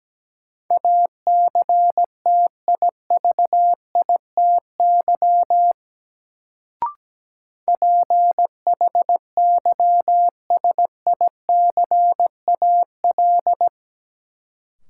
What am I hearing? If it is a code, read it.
Morse code, heard as a keyed tone, about 17 wpm, 700 Hz. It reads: ACTIVITY PHYSICAL
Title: Sentences from Top 1000 Words - Review Part 4 - New Word Drill - 17wpm